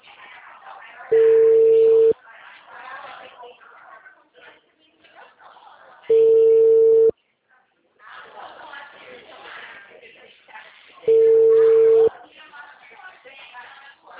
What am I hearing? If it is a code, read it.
Ringback tone (China, 450 Hz), with speech.